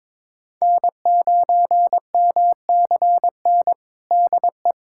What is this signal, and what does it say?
Morse code, heard as a keyed tone, about 22 wpm, 700 Hz. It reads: N9MCN DE
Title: USA Callsign DE Callsign - 5 Characters - 22wpm